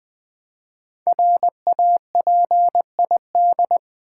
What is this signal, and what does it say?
Morse code, heard as a keyed tone, about 20 wpm, 700 Hz. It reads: RAPID
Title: Build A Word - 1 Letter At A Time - 5-Letter Words - 20wpm